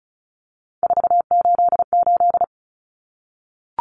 Morse code, heard as a keyed tone, about 34 wpm, 700 Hz. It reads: 488
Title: Common Cut Numbers - 3-Digit 35wpm